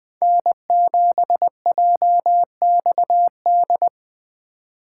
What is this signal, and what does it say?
Morse code, heard as a keyed tone, about 24 wpm, 700 Hz. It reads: N7JXD
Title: Speed Racing - USA 1x3 Call Signs 20wpm